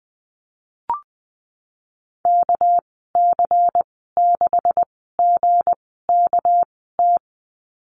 Morse code, heard as a keyed tone, about 20 wpm, spaced as for 15 wpm, 700 Hz. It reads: KC6GKT
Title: 207 Lesson 33g - Letter j - Call Signs 20wpm